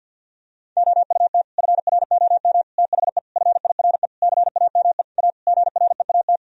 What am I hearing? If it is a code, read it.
Morse code, heard as a keyed tone, about 49 wpm, 700 Hz. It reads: OUT FROM THE FIRE CAME A GREAT
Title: Sentences from Top 1000 Words - Review Part 3 - 50wpm